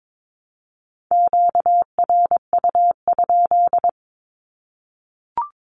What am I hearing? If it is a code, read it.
Morse code, heard as a keyed tone, about 22 wpm, 700 Hz. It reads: QRU?
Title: Q-Codes Repeated 3x 22wpm